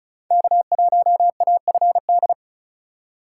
Morse code, heard as a keyed tone, about 36 wpm, 700 Hz. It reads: K1AFD
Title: USA Call Signs - 4x Character Spacing - 35wpm